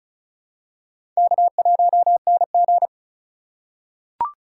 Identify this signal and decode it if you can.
Morse code, heard as a keyed tone, about 34 wpm, 700 Hz. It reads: K1NG
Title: USA Call Signs - 4 Characters - 4x Character Spacing - 35wpm